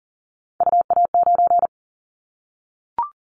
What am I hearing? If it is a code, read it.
Morse code, heard as a keyed tone, about 40 wpm, 700 Hz. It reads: UA9
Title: Callsign Prefixes for Top 50 DXCC - 40wpm